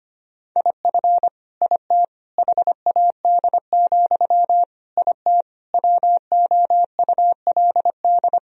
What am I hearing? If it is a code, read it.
Morse code, heard as a keyed tone, about 25 wpm, 700 Hz. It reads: IF IT HAD, IT WOULD
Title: Sentences from Top 1000 Words - Review Part 3 - 25wpm